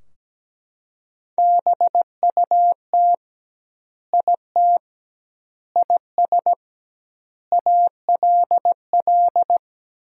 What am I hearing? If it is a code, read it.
Morse code, heard as a keyed tone, about 17 wpm, 700 Hz. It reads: BUT IT IS ALL
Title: 2x Word-Spacing - Sentences from Top 300 Words - 17wpm